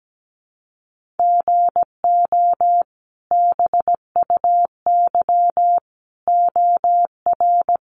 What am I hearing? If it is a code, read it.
Morse code, heard as a keyed tone, about 17 wpm, 700 Hz. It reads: GO BUY OR
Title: Sentences from Top 1500 Words 17wpm